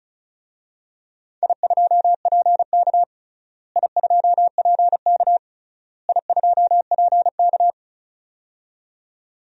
Morse code, heard as a keyed tone, about 34 wpm, 700 Hz. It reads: I2PK I2PK I2PK
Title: DXCC-100 Call Signs - 4 Characters - Repeated 3x - 35wpm